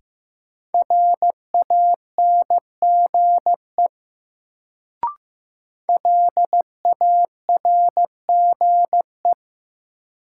Morse code, heard as a keyed tone, about 15 wpm, 700 Hz. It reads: RANGE LARGE
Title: Sets of 4 Words - New Word Drill - 15wpm